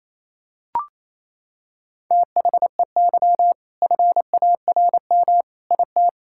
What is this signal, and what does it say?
Morse code, heard as a keyed tone, about 28 wpm, 700 Hz. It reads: THEY FARM IT
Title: Sentences from Top 1000 Words - Review Part 5 - 28wpm